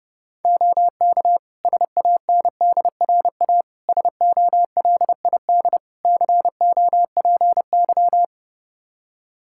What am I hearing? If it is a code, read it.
Morse code, heard as a keyed tone, about 32 wpm, 700 Hz. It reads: OK SANDRA SOLID COPY
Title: Practice QSO - Exchange 3 - More Info - 30wpm